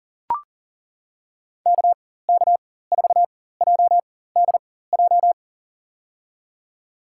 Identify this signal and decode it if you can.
Morse code, heard as a keyed tone, about 40 wpm, spaced as for 19 wpm, 700 Hz. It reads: KK4JDJ
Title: USA Call Signs - 4x Character Spacing - 40wpm